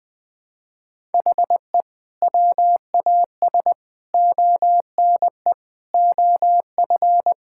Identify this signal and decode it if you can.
Morse code, heard as a keyed tone, about 20 wpm, 700 Hz. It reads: HE WAS ONE OF
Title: Sentences with 3-Word Phrases - 20wpm